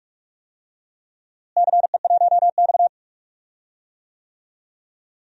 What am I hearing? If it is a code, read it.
Morse code, heard as a keyed tone, about 45 wpm, 700 Hz. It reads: CE1X
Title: DXCC-100 Call Signs - 4 Characters - Repeated 2x - 45wpm